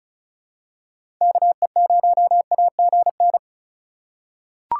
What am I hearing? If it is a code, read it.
Morse code, heard as a keyed tone, about 36 wpm, 700 Hz. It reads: KE0AGN
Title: USA Call Signs - 4x Character Spacing - 35wpm